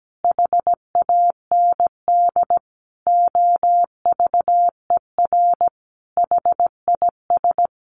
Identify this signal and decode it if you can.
Morse code, heard as a keyed tone, about 17 wpm, 700 Hz. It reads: HAND OVER HIS
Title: Sentences from Top 1200 Words 17wpm